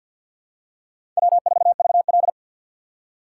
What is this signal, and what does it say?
Morse code, heard as a keyed tone, about 48 wpm, 700 Hz. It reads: W4VL
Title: USA Call Signs - 4x Character Spacing - 50wpm